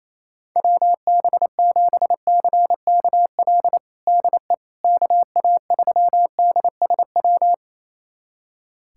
Morse code, heard as a keyed tone, about 28 wpm, 700 Hz. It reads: WB7CKL DE KA3DSW
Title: USA Callsign DE Callsign - 6 Characters - 28wpm